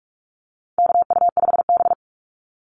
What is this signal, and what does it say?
Morse code, heard as a keyed tone, about 44 wpm, 700 Hz. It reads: KU5B
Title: Call Signs - 45wpm